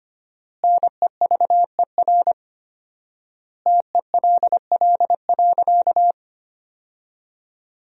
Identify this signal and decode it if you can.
Morse code, heard as a keyed tone, about 25 wpm, 700 Hz. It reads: NEVER TELL.
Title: 4x Word-Spacing - Sentences from Top 300 Words - 25wpm